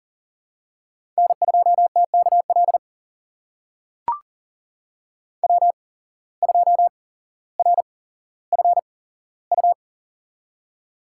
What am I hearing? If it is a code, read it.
Morse code, heard as a keyed tone, about 40 wpm, spaced as for 17 wpm, 700 Hz. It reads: N2TKL W2RFU
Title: USA Call Signs - 5 Characters - 8x Character Spacing - 40wpm